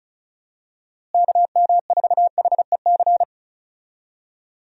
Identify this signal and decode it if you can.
Morse code, heard as a keyed tone, about 34 wpm, 700 Hz. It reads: KM4HEC
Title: USA Call Signs - 6 Characters - Repeated 2x - 35wpm